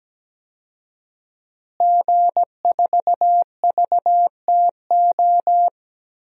Morse code, heard as a keyed tone, about 17 wpm, 700 Hz. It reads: G4VTO
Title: DXCC-100 Call Signs - 5 Characters - Repeated 3x - 17wpm